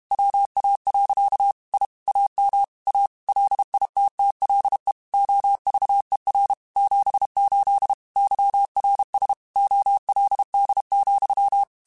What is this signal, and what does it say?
Morse code, heard as a keyed tone, about 32 wpm, 800 Hz. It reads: WA. I AM A LITTLE OVER 78 YRS OLD,
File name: QSOs / QSOs-mp3s / WA7QOX 2@30